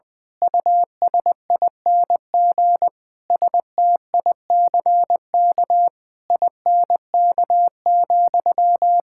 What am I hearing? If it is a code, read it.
Morse code, heard as a keyed tone, about 20 wpm, 700 Hz. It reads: USING STICK INK,
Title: Pangram Challenge v2 - 20wpm